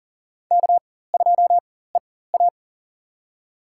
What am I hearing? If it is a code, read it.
Morse code, heard as a keyed tone, about 40 wpm, spaced as for 19 wpm, 700 Hz. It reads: K2EA
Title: USA Call Signs - 4x Character Spacing - 40wpm